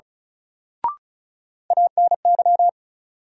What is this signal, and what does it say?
Morse code, heard as a keyed tone, about 36 wpm, 700 Hz. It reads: ANY
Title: Sets of 3 Words - New Word Drill - 35wpm